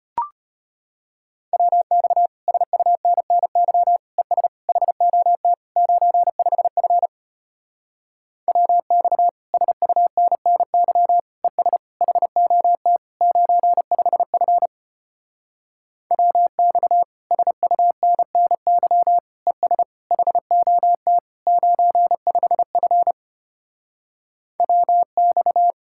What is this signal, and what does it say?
Morse code, heard as a keyed tone, about 33 wpm, 700 Hz. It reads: WX SUNNY ES HOT 95F WX SUNNY ES HOT 95F WX SUNNY ES HOT 95F WX